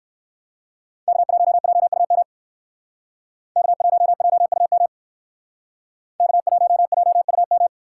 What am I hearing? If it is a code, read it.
Morse code, heard as a keyed tone, about 56 wpm, 700 Hz. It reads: K1JUM K1JUM K1JUM
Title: Speed Racing - USA 1x3 Call Signs 45wpm